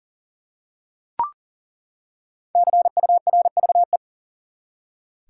Morse code, heard as a keyed tone, about 39 wpm, 700 Hz. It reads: CURVE
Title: Five Letter Words - 40wpm